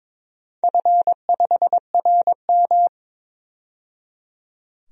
Morse code, heard as a keyed tone, about 22 wpm, 700 Hz. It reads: F5RM
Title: DXCC-100 Call Signs - 4 Characters - 22wpm